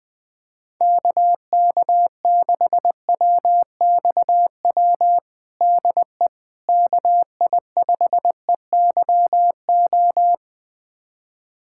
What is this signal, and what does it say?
Morse code, heard as a keyed tone, about 20 wpm, 700 Hz. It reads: KK6WXW DE KI5EYO
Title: USA Callsign DE Callsign - 6 Characters - 20wpm